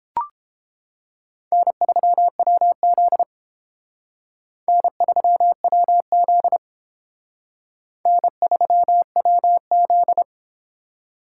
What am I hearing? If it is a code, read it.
Morse code, heard as a keyed tone, about 29 wpm, 700 Hz. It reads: N3WZ N3WZ N3WZ